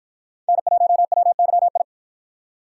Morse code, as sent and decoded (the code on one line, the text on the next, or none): -. .---- .-- -.-- ..
N1WYI